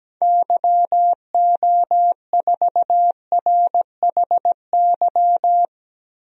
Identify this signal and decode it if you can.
Morse code, heard as a keyed tone, about 17 wpm, 700 Hz. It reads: YO4RHY
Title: DXCC-100 Call Signs - 6 Characters - 17wpm